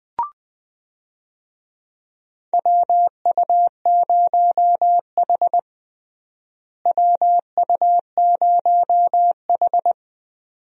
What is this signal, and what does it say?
Morse code, heard as a keyed tone, about 20 wpm, 700 Hz. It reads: WU0H WU0H
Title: USA Call Signs - 4 Characters - Repeated 2x - 20wpm